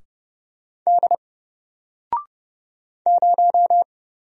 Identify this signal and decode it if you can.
Morse code, heard as a keyed tone, about 30 wpm, 700 Hz. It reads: D 0
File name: Single-Letter-Number-30wpm